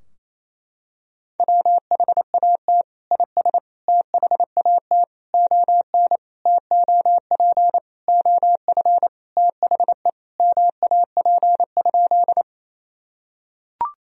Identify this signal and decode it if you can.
Morse code, heard as a keyed tone, about 28 wpm, 690 Hz. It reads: WHAT IS THAT ON TOP OF THE MAP?
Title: Sentences from Top 1000 Words - Review Part 2 - 28wpm